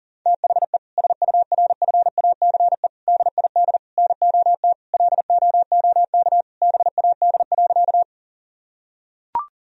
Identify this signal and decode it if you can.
Morse code, heard as a keyed tone, about 37 wpm, 700 Hz. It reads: THE SURFACE DID NOT LOOK BAD.
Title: Sentences from Top 1000 Words - Review Part 3 - 40wpm